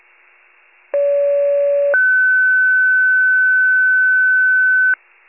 Two-tone page, heard as a one-second tone, then a three-second tone, about 570 then 1500 Hz. It is heard from 0.9 to 4.9 s.
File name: pager / 29 FIRE / wcalert